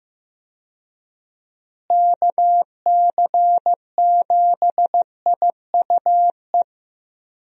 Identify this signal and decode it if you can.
Morse code, heard as a keyed tone, about 15 wpm, 700 Hz. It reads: KC7IUE
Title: USA Call Signs - 6 Characters - Repeated 3x - 15wpm